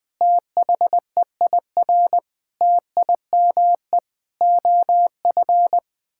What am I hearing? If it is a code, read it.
Morse code, heard as a keyed tone, about 20 wpm, 700 Hz. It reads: THEIR TIME OF
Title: Sentences from Top 1000 Words - Review Part 5 - 20wpm